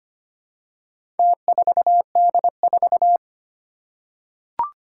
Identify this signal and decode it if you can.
Morse code, heard as a keyed tone, about 25 wpm, 700 Hz. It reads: T4D4
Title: Four Challenging Letter-Number Combinations 25wpm